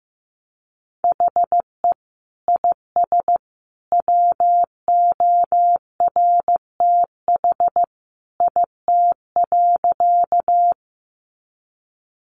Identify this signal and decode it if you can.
Morse code, heard as a keyed tone, about 15 wpm, 700 Hz. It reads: HE IS WORTH IT.
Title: Sentences from Top 800 Words 15wpm